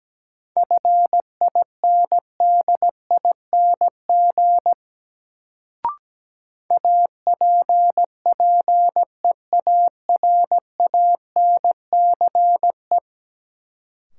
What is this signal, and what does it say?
Morse code, heard as a keyed tone, about 17 wpm, 700 Hz. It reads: FINDING APPEARANCE
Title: Sentences from Top 1100 Words - New Word Drill - 17wpm